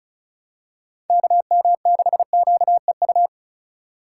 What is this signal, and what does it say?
Morse code, heard as a keyed tone, about 35 wpm, 700 Hz. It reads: KM6QEU
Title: USA Call Signs - 6 Characters - 3x Character Spacing - 35wpm